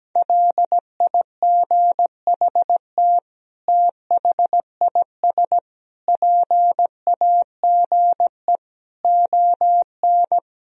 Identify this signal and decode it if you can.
Morse code, heard as a keyed tone, about 17 wpm, 700 Hz. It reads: LIGHT THIS PAGE ON
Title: Sentences from Top 1000 Words - Review Part 1 - 17wpm